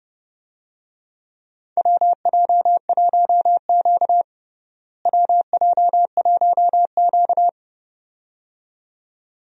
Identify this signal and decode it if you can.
Morse code, heard as a keyed tone, about 31 wpm, 700 Hz. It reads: WJ1Q WJ1Q
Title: USA Call Signs - 4 Characters - Repeated 2x - 30wpm